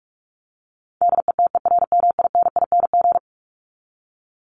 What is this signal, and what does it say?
Morse code, heard as a keyed tone, about 44 wpm, 700 Hz. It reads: DETERMINING
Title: ING Suffix - Top 100 Words with Base Words 45wpm